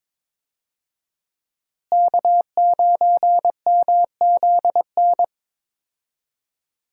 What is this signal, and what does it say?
Morse code, heard as a keyed tone, about 22 wpm, 700 Hz. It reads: K9MZN
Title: USA Call Signs - 5 Characters - Repeated 3x - 22wpm